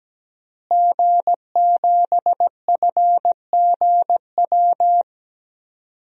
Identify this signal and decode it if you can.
Morse code, heard as a keyed tone, about 17 wpm, 700 Hz. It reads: G7FGW
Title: DXCC-100 Call Signs - 5 Characters - Repeated 2x - 17wpm